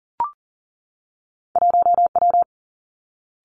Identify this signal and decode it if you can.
Morse code, heard as a keyed tone, about 39 wpm, 700 Hz. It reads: 1W
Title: Two Challenging Letter-Number Combinations 40wpm